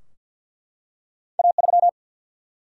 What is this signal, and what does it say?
Morse code, heard as a keyed tone, about 50 wpm, 700 Hz. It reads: A3
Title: Single-Letter Single-Number 50wpm